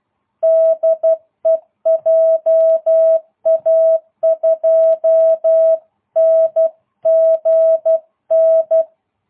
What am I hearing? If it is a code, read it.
Morse code, heard as a keyed tone, about 12 wpm, 660 Hz. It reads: DE JA2NGN